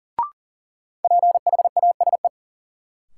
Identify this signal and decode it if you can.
Morse code, heard as a keyed tone, about 37 wpm, 700 Hz. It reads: PHASE
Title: Sentences from Top 1200 Words - New Word Drill - 40wpm